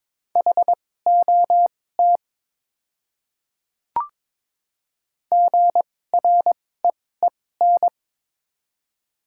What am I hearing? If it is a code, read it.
Morse code, heard as a keyed tone, about 22 wpm, spaced as for 16 wpm, 700 Hz. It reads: HOT GREEN
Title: 2x Character Spacing - Top 500 Words - 22wpm